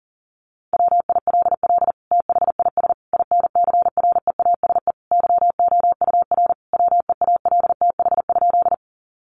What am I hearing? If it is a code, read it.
Morse code, heard as a keyed tone, about 40 wpm, 700 Hz. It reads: WILL THIS INCREASE YOUR WEALTH?
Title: Sentences from Top 1200 Words 40wpm